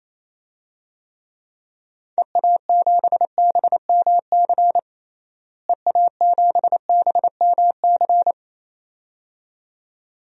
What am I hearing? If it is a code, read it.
Morse code, heard as a keyed tone, about 28 wpm, 700 Hz. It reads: EA7BMC EA7BMC
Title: DXCC-100 Call Signs - 6 Characters - Repeated 2x - 28wpm